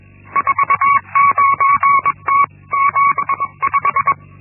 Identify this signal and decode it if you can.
Morse code, heard as a keyed tone, about 21 wpm, 1120 Hz. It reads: 49T75
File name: Linksmakalnis-1